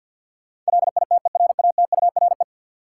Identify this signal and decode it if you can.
Morse code, heard as a keyed tone, about 50 wpm, 700 Hz. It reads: LITERATURE
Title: Sentences from Top 600 Words - New Word Drill - 50wpm